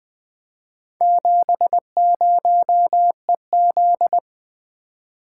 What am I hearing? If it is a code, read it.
Morse code, heard as a keyed tone, about 20 wpm, 700 Hz. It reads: 70EZ
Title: Four Letter-Number Combinations - 20wpm